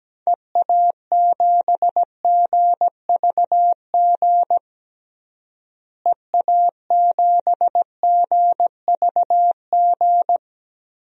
Morse code, heard as a keyed tone, about 17 wpm, 700 Hz. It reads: EA7GVG EA7GVG
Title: DXCC-100 Call Signs - 6 Characters - Repeated 3x - 17wpm